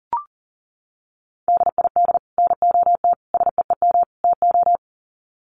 Morse code, heard as a keyed tone, about 40 wpm, 700 Hz. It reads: DID NOT SEEM TO
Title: Sets of 4 Words 40wpm